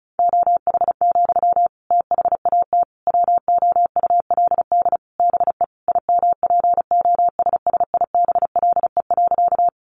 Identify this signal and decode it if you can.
Morse code, heard as a keyed tone, about 34 wpm, 700 Hz. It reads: OH, THAT WOULD BE IMPOSSIBLE.
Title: Sentences from Top 1100 Words 35wpm